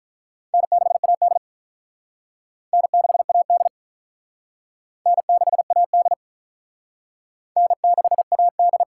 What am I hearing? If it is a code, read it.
Morse code, heard as a keyed tone, about 48 wpm, 700 Hz. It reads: N6AD N6AD N6AD N6AD